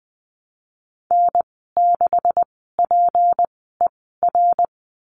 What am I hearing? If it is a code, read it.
Morse code, heard as a keyed tone, about 20 wpm, spaced as for 14 wpm, 700 Hz. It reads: N6PER
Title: 186 Lesson 30g - Letter b - Call Signs 20wpm